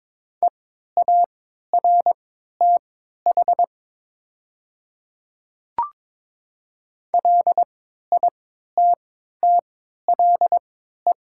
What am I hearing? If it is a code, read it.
Morse code, heard as a keyed tone, about 22 wpm, spaced as for 12 wpm, 700 Hz. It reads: EARTH LITTLE
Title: 3x Character Spacing - Top 200 Words - 22wpm